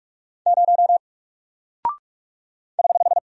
Morse code, heard as a keyed tone, about 44 wpm, 700 Hz. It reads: 0 <HH>
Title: Single Character (Letter:Number:Punctuation:Prosign) 45wpm